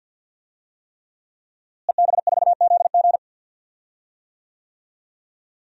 Morse code, heard as a keyed tone, about 49 wpm, 700 Hz. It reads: EB4ZG
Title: DXCC-100 Call Signs - 5 Characters - Repeated 3x - 50wpm